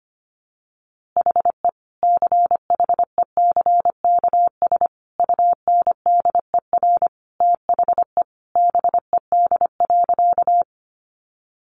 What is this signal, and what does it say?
Morse code, heard as a keyed tone, about 25 wpm, 700 Hz. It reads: HE CHECKS UNDER THE BED.
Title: Easy Phrases 25wpm